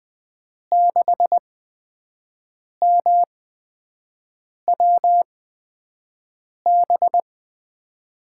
Morse code, heard as a keyed tone, about 20 wpm, spaced as for 5 wpm, 700 Hz. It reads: 6MWB
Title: USA Call Signs - 5 Characters - 8x Character Spacing - 20wpm